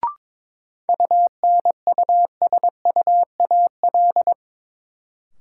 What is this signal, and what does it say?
Morse code, heard as a keyed tone, about 22 wpm, 700 Hz. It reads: UNUSUAL
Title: Sentences from Top 1400 Words - New Word Drill - 22wpm